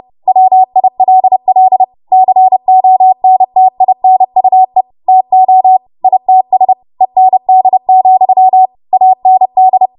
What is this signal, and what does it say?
Morse code, heard as a keyed tone, about 30 wpm, 750 Hz. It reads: WILL CONTINUE TO ITS END, AND
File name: Morse / 30WPM